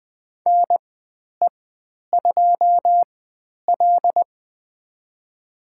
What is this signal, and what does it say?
Morse code, heard as a keyed tone, about 20 wpm, spaced as for 10 wpm, 700 Hz. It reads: NE2L
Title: 109 Lesson 18g - Letter m - Call Signs 20wpm